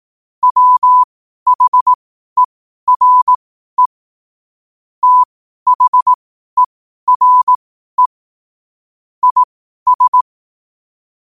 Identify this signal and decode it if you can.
Morse code, heard as a keyed tone, about 18 wpm, spaced as for 12 wpm, 1000 Hz. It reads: WHERE THERE IS
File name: qotdfiles / 12wpm